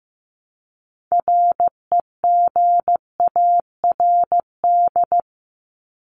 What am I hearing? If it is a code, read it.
Morse code, heard as a keyed tone, about 15 wpm, 700 Hz. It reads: REGARD
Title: ING Suffix - Top 100 Words with Base Words 15wpm